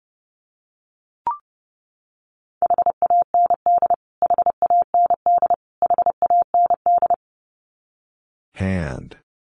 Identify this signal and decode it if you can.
Morse code, heard as a keyed tone, about 30 wpm, 700 Hz. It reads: HAND HAND HAND
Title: Top 200 Words Repeated 30wpm